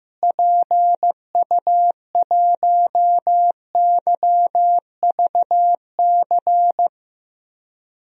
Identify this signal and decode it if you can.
Morse code, heard as a keyed tone, about 15 wpm, 700 Hz. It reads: PU1YVC
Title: DXCC-100 Call Signs - 6 Characters - Repeated 3x - 15wpm